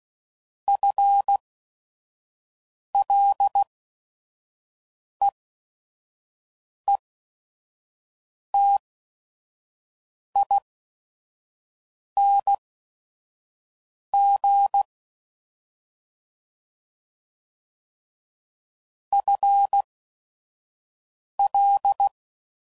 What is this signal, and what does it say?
Morse code, heard as a keyed tone, about 16 wpm, spaced as for 5 wpm, 800 Hz. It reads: FLEETING FL
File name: Prefixes / FLs & THs / FL Prefixes 1